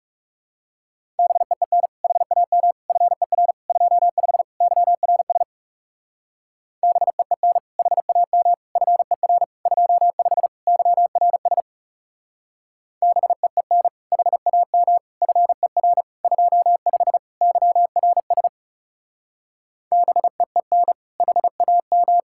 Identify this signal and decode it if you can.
Morse code, heard as a keyed tone, about 36 wpm, 700 Hz. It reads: BEEN HAM FER 25 YRS BEEN HAM FER 25 YRS BEEN HAM FER 25 YRS BEEN HAM
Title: Speed-Racing Common QSO Phrases - 30wpm